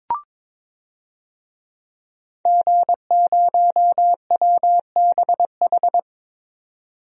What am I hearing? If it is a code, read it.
Morse code, heard as a keyed tone, about 22 wpm, 700 Hz. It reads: G0WBH
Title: DXCC-100 Call Signs - 5 Characters - 22wpm